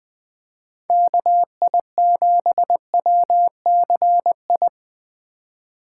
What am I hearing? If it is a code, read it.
Morse code, heard as a keyed tone, about 20 wpm, 700 Hz. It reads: KI7WCI
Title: Speed Racing - USA 2x3 Call Signs 17wpm